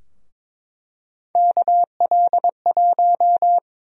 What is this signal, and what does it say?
Morse code, heard as a keyed tone, about 22 wpm, 700 Hz. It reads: KL1
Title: USA Call Sign Prefixes - Repeated 2x - 22wpm